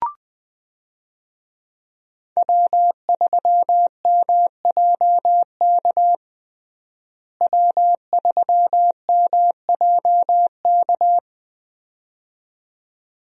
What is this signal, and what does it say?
Morse code, heard as a keyed tone, about 20 wpm, 700 Hz. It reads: W3MJK W3MJK
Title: USA Call Signs - 5 Characters - Repeated 2x - 20wpm